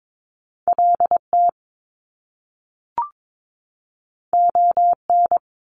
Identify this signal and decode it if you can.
Morse code, heard as a keyed tone, about 22 wpm, 700 Hz. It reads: LT ON
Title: Top Two Letter Combinations in Words 22wpm